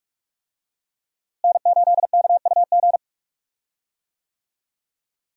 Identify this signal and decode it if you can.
Morse code, heard as a keyed tone, about 46 wpm, 700 Hz. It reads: N8KUG
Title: USA Call Signs - 5 Characters - Repeated 3x - 45wpm